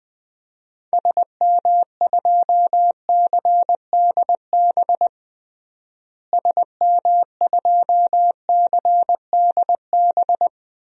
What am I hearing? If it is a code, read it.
Morse code, heard as a keyed tone, about 20 wpm, 700 Hz. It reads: SM2CDB SM2CDB
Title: DXCC-100 Call Signs - 6 Characters - Repeated 2x - 20wpm